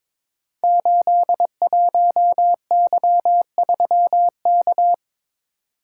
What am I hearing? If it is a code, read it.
Morse code, heard as a keyed tone, about 22 wpm, 700 Hz. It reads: 81Y3K
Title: Code Groups - 5 Letter:Numbers - 22wpm